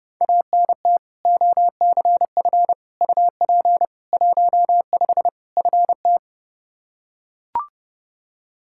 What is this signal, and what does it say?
Morse code, heard as a keyed tone, about 32 wpm, 700 Hz. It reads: ANT OCF UP 15 FT
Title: Practice QSO - Exchange 2 - Info - 30wpm